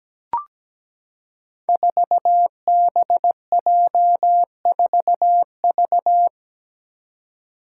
Morse code, heard as a keyed tone, about 17 wpm, 700 Hz. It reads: 4BJ4V
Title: Five Challenging Letter-Number Combinations 17wpm